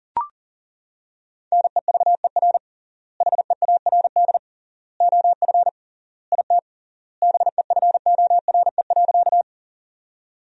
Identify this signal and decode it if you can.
Morse code, heard as a keyed tone, about 40 wpm, 700 Hz. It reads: NEVER HEARD OF IT BEFORE.
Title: 3x Word-Spacing - Sentences from Top 400 Words - 40wpm